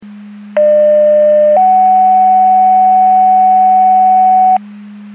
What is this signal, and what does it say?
Two-tone page, heard as a one-second tone, then a three-second tone, about 600 then 750 Hz. It is heard from 0.6 to 4.6 s.
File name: Portage 154.385 Paging